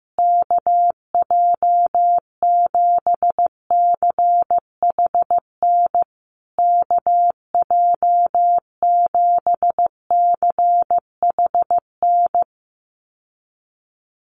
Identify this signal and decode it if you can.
Morse code, heard as a keyed tone, about 15 wpm, 700 Hz. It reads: KJ7CHN KJ7CHN
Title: USA Call Signs Repeated 3x 15wpm